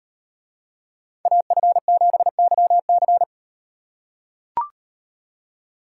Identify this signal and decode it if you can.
Morse code, heard as a keyed tone, about 35 wpm, 700 Hz. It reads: AF7YC